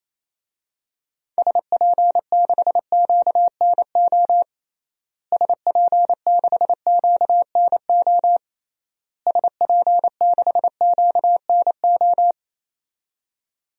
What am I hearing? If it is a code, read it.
Morse code, heard as a keyed tone, about 28 wpm, 700 Hz. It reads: SP6QNO SP6QNO SP6QNO
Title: DXCC-100 Call Signs - 6 Characters - Repeated 3x - 28wpm